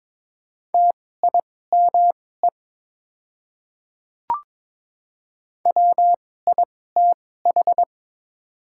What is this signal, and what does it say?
Morse code, heard as a keyed tone, about 21 wpm, spaced as for 16 wpm, 700 Hz. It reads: TIME WITH
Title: 2x Character Spacing - Top 200 Words - 22wpm